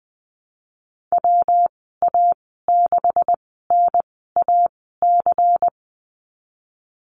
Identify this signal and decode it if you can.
Morse code, heard as a keyed tone, about 20 wpm, spaced as for 15 wpm, 700 Hz. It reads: WA6NAC
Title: 179 Lesson 29g - Slash - Call Signs 20wpm